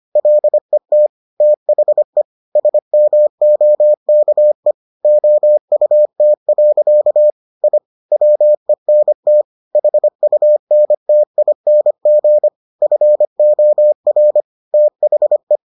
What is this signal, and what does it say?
Morse code, heard as a keyed tone, about 25 wpm, 600 Hz. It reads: LET THE SMOKE OUT. I WENT HUNTING FOR THE